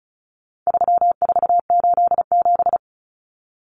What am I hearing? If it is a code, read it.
Morse code, heard as a keyed tone, about 34 wpm, 700 Hz. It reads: 3487
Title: Numbers - 4-Digit 35wpm